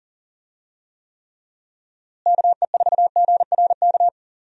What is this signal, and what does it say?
Morse code, heard as a keyed tone, about 40 wpm, 700 Hz. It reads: KE4GRK
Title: USA Call Signs - 6 Characters - Repeated 2x - 40wpm